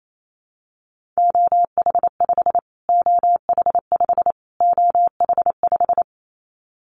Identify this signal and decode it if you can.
Morse code, heard as a keyed tone, about 28 wpm, 700 Hz. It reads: OH5 OH5 OH5
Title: Callsign Prefixes for Top 25 DXCC - Repeated 3x - 28wpm